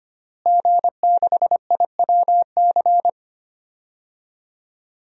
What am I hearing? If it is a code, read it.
Morse code, heard as a keyed tone, about 25 wpm, 700 Hz. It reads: G6IWC
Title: DXCC-100 Call Signs - 5 Characters - Repeated 3x - 25wpm